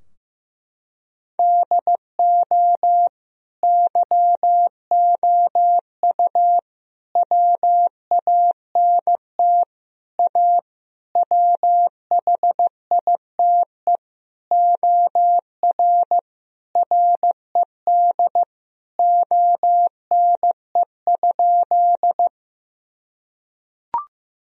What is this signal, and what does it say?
Morse code, heard as a keyed tone, about 16 wpm, 700 Hz. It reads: DO YOU WANT A WHITE OR RED ONE?
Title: Sentences from Top 1000 Words - Review Part 2 - 15wpm